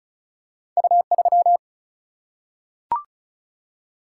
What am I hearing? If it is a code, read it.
Morse code, heard as a keyed tone, about 36 wpm, 700 Hz. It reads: U 3
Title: Single-Letter Single-Number 35wpm